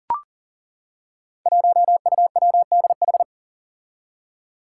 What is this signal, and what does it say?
Morse code, heard as a keyed tone, about 40 wpm, 700 Hz. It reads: 1UWDH
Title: Five Challenging Letter-Number Combinations 40wpm